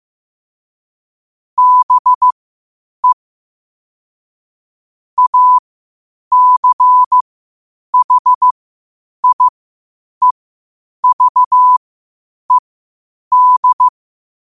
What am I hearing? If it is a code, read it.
Morse code, heard as a keyed tone, about 15 wpm, spaced as for 8 wpm, 1000 Hz. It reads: BE ACHIEVED
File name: qotdfiles / 8wpm